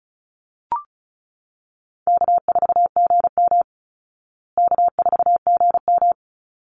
Morse code, heard as a keyed tone, about 34 wpm, 700 Hz. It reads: K4GM K4GM
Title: Easy Call Signs Repeated 2x - 35wpm